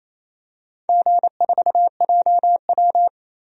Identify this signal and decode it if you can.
Morse code, heard as a keyed tone, about 28 wpm, 700 Hz. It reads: G4JW
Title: DXCC-100 Call Signs - 4 Characters - 28wpm